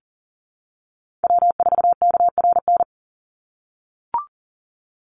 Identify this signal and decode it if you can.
Morse code, heard as a keyed tone, about 41 wpm, 700 Hz. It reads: W4KRN
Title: NA SOTA Chaser Callsigns 40wpm